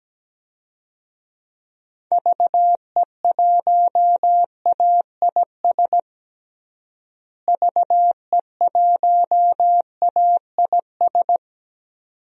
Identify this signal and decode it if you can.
Morse code, heard as a keyed tone, about 17 wpm, 700 Hz. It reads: VE1AIS VE1AIS
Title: DXCC-100 Call Signs - 6 Characters - Repeated 2x - 17wpm